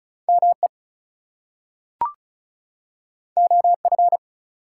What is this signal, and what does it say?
Morse code, heard as a keyed tone, about 36 wpm, 700 Hz. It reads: ME OF
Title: Two Letter Words - 35wpm